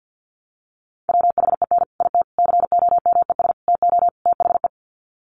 Morse code, heard as a keyed tone, about 50 wpm, 700 Hz. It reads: WHEN IT COMES TO THE
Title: Sets of 5 Words 50wpm